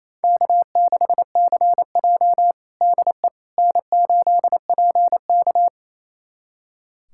Morse code, heard as a keyed tone, about 28 wpm, 700 Hz. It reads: K6CJ DE N8PK
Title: USA Callsign DE Callsign - 4 Characters - 28wpm